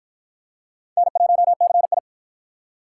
Morse code, heard as a keyed tone, about 52 wpm, 700 Hz. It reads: N1CI